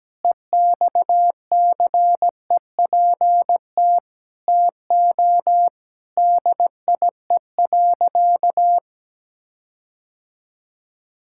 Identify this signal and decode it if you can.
Morse code, heard as a keyed tone, about 17 wpm, 700 Hz. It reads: EXCEPT TO DIE.
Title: Sentences from Top 1600 Words 17wpm